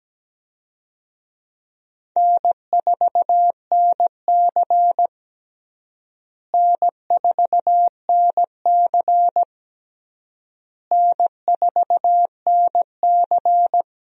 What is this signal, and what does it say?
Morse code, heard as a keyed tone, about 17 wpm, 700 Hz. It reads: N4NC N4NC N4NC
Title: USA Call Signs - 4 Characters - Repeated 3x - 17wpm